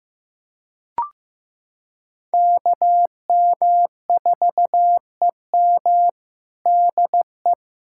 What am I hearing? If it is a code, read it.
Morse code, heard as a keyed tone, about 15 wpm, 700 Hz. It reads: KM4EM DE
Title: Practice QSO - Exchange 4 - Ending - 15wpm